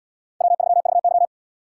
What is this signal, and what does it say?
Morse code, heard as a keyed tone, about 77 wpm, 700 Hz. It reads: W3VY